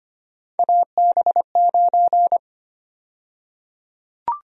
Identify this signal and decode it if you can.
Morse code, heard as a keyed tone, about 25 wpm, 700 Hz. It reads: AB9
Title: USA Call Sign Prefixes - Repeated 2x - 25wpm